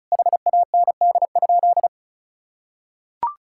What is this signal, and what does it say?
Morse code, heard as a keyed tone, about 34 wpm, 700 Hz. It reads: HAND?
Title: Sentences from Top 1000 Words - Review Part 5 - 35wpm